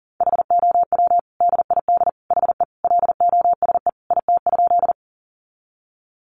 Morse code, heard as a keyed tone, about 40 wpm, 700 Hz. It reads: HOW DID HE LOSE IT?
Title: Sentences from Top 1100 Words 40wpm